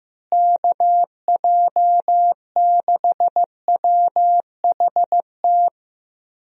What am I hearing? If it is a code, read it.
Morse code, heard as a keyed tone, about 15 wpm, 700 Hz. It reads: KJ6WHT
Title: USA Call Signs - 6 Characters - Repeated 3x - 15wpm